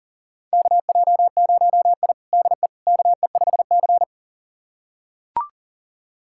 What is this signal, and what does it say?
Morse code, heard as a keyed tone, about 39 wpm, 700 Hz. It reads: KJ0I DE KE5C
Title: USA Callsign DE Callsign - 4 Characters - 40wpm